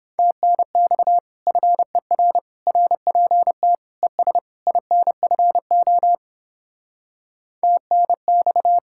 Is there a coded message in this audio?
Yes: Morse code from 0.2 to 8.8 s, heard as a keyed tone, about 29 wpm, 700 Hz. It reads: TNX FER RPT ES INFO TNX